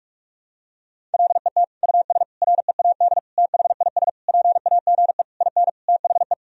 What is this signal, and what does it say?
Morse code, heard as a keyed tone, about 41 wpm, 700 Hz. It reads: LET US READ THIS PAGE IN THE
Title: Sentences from Top 1000 Words - Review Part 1 - 45wpm